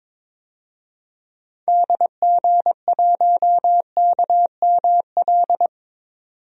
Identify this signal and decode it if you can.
Morse code, heard as a keyed tone, about 22 wpm, 700 Hz. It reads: DG1KML
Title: DXCC-100 Call Signs - 6 Characters - 22wpm